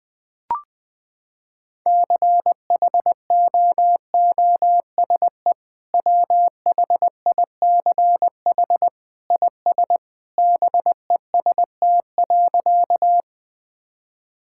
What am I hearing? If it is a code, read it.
Morse code, heard as a keyed tone, about 21 wpm, 700 Hz. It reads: CHOOSE WHICH IS BEST.
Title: Sentences with 2-Word Phrases - 20wpm